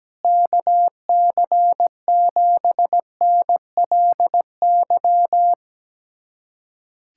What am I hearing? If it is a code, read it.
Morse code, heard as a keyed tone, about 17 wpm, 700 Hz. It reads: KC7NLY
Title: USA Callsign DE Callsign 17wpm